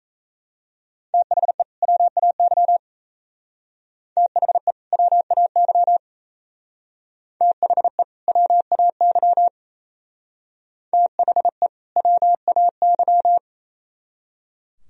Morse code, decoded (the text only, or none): THE WAY THE WAY THE WAY THE WAY